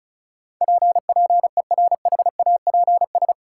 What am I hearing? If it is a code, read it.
Morse code, heard as a keyed tone, about 34 wpm, 700 Hz. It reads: P PERHAPS
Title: Sentences from Top 1100 Words - New Word Drill Alt - 35wpm